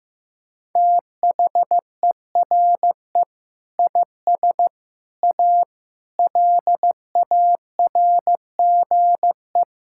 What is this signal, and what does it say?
Morse code, heard as a keyed tone, about 15 wpm, 700 Hz. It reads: THERE IS A LARGE
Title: Sentences from Top 1000 Words - Review Part 5 - 15wpm